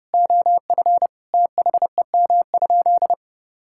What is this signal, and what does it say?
Morse code, heard as a keyed tone, about 32 wpm, 700 Hz. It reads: OF THEM?
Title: Sentences from Top 1000 Words - Review Part 6 - 30wpm